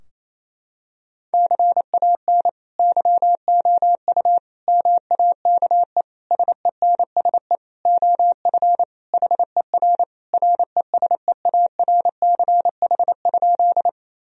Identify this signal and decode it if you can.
Morse code, heard as a keyed tone, about 28 wpm, 700 Hz. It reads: CAN YOU MAKE SENSE OF HER RESEARCH?
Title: Sentences from Top 1000 Words - Review Part 4 - 28wpm